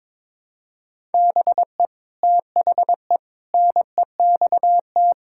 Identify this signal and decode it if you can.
Morse code, heard as a keyed tone, about 22 wpm, 700 Hz. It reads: BE THE NEXT
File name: practice / 22wpm / Sets of 8 Words - 22wpm